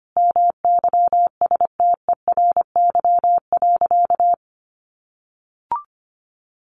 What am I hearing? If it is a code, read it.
Morse code, heard as a keyed tone, about 25 wpm, 700 Hz. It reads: MYSTERY.
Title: A Sound Of Thunder 25wpm